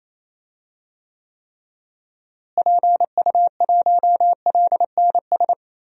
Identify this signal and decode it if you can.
Morse code, heard as a keyed tone, about 28 wpm, 700 Hz. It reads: PU1LNS
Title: DXCC-100 Call Signs - 6 Characters - Repeated 3x - 28wpm